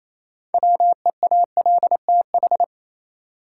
Morse code, heard as a keyed tone, about 28 wpm, 700 Hz. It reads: WEALTH
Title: Sentences from Top 1200 Words - New Word Drill Alt - 28wpm